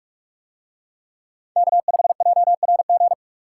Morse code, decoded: K5JRG

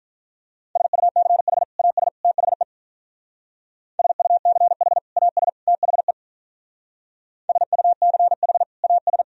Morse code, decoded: SUCH AS THE SUCH AS THE SUCH AS

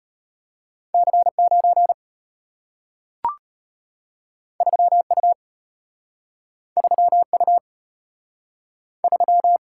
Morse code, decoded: C9 3U 3U 3

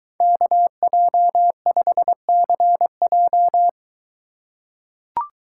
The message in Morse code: KJ5CJ

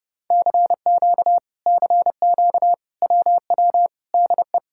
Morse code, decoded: CQ CQ WW DE

